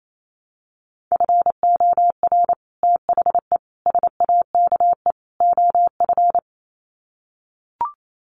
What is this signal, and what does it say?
Morse code, heard as a keyed tone, about 28 wpm, 700 Hz. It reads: FOR THE SAKE OF
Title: Sets of 4 Words 28wpm